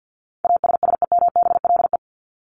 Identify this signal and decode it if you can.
Morse code, heard as a keyed tone, about 48 wpm, 700 Hz. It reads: ASSEMBLE
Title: LY Suffix - Top 100 Words with Base Words 50wpm